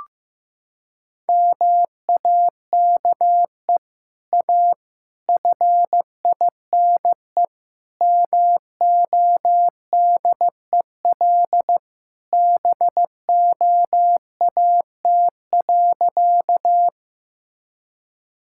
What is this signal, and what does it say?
Morse code, heard as a keyed tone, about 15 wpm, 700 Hz. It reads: MAKE A FINE MODEL BOAT.
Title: Sentences from Top 1000 Words - Review Part 5 - 15wpm